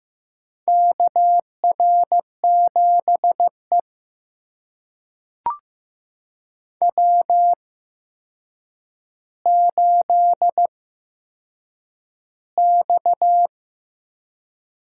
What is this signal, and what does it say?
Morse code, heard as a keyed tone, about 15 wpm, spaced as for 7 wpm, 700 Hz. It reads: KR7E W8X
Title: USA Call Signs - 4 Characters - 8x Character Spacing - 15wpm